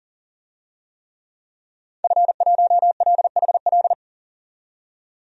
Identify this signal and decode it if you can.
Morse code, heard as a keyed tone, about 40 wpm, 700 Hz. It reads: F1LHL
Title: DXCC-100 Call Signs - 5 Characters - 40wpm